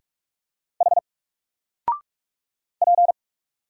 Morse code, heard as a keyed tone, about 44 wpm, 700 Hz. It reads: H P
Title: Single Character (Letter:Number:Punctuation:Prosign) 45wpm